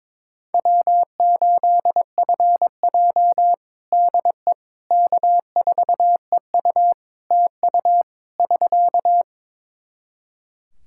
Morse code, heard as a keyed tone, about 22 wpm, 700 Hz. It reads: W8FJ DE K4EU TU <SK>
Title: Practice QSO - Exchange 4 - Ending - 22wpm